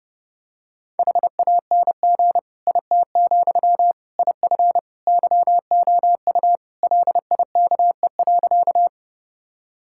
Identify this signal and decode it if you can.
Morse code, heard as a keyed tone, about 30 wpm, 700 Hz. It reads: HANG IT, IF YOU LIKE.
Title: Sentences from Top 1600 Words 30wpm